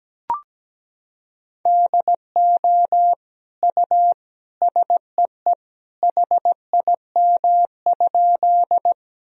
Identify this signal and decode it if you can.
Morse code, heard as a keyed tone, about 17 wpm, 700 Hz. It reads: DO U SEE HIM?
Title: Sentences from Top 100 Words and Abbreviations 17wpm